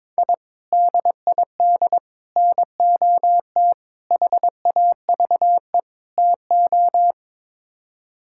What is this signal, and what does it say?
Morse code, heard as a keyed tone, about 22 wpm, 700 Hz. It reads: I DID NOT HAVE TO
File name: Build A Phrase - 5-Word Phrases - 22wpm